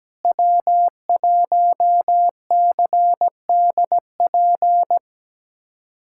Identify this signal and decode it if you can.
Morse code, heard as a keyed tone, about 17 wpm, 700 Hz. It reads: W1CDP
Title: USA Callsign DE Callsign 17wpm